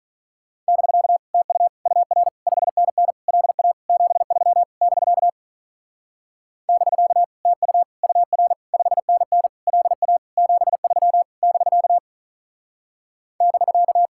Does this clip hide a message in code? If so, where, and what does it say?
Yes: Morse code from 0.7 to 14.1 s, heard as a keyed tone, about 45 wpm, 700 Hz. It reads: <BK> TU UR 5NN LA 73 <BK> <BK> TU UR 5NN LA 73 <BK> <BK>